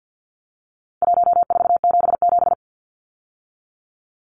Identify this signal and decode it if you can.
Morse code, heard as a keyed tone, about 50 wpm, 700 Hz. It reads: 1477
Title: Numbers - Contest Style - 4-Digit 50wpm